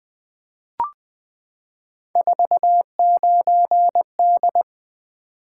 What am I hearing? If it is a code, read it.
Morse code, heard as a keyed tone, about 20 wpm, 700 Hz. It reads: 49D
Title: Three Letter-Number Combinations - 20wpm